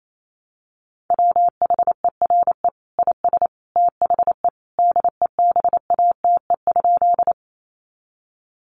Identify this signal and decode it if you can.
Morse code, heard as a keyed tone, about 28 wpm, 700 Hz. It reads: WHERE IS THE DEBATE?
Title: Sentences from Top 1100 Words 28wpm